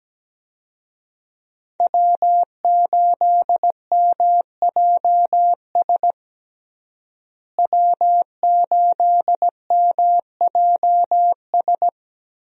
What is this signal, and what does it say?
Morse code, heard as a keyed tone, about 17 wpm, 700 Hz. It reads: W8MJS W8MJS
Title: USA Call Signs - 5 Characters - Repeated 2x - 17wpm